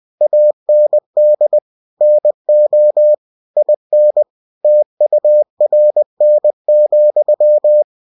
Morse code, heard as a keyed tone, about 20 wpm, 600 Hz. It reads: AND NO IN TURN,
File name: cw-books / winnie-the-pooh / winnie-20wpm-page-0031